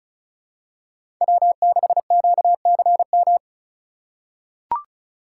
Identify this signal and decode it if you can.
Morse code, heard as a keyed tone, about 36 wpm, 700 Hz. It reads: W6QCM
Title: USA Call Signs - 5 Characters - 2x Character Spacing - 35wpm